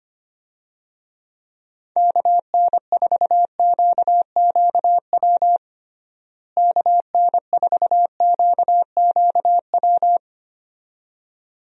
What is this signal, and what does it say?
Morse code, heard as a keyed tone, about 25 wpm, 700 Hz. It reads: KN4QQW KN4QQW
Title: USA Call Signs - 6 Characters - Repeated 2x - 25wpm